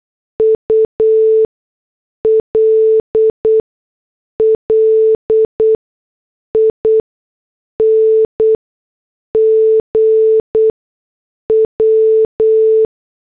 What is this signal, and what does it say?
Morse code, heard as a keyed tone, about 8 wpm, spaced as for 6 wpm, 440 Hz. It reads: ULLING W